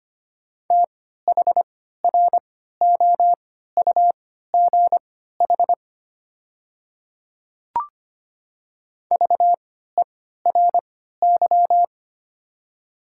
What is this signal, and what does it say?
Morse code, heard as a keyed tone, about 25 wpm, spaced as for 14 wpm, 700 Hz. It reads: THROUGH VERY
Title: 3x Character Spacing - Top 200 Words - 25wpm